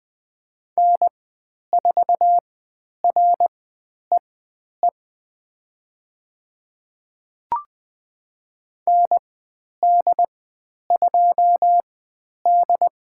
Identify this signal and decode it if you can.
Morse code, heard as a keyed tone, about 20 wpm, spaced as for 10 wpm, 700 Hz. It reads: N4REE ND2D
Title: 095 Lesson 16g - Letter c - Call Signs 20wpm